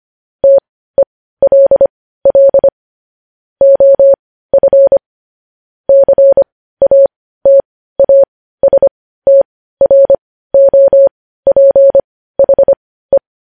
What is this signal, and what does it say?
Morse code, heard as a keyed tone, about 25 wpm, spaced as for 15 wpm, 550 Hz. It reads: TELL OF CATASTROPHE
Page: news-headlines-in-morse-code-at-15-wpm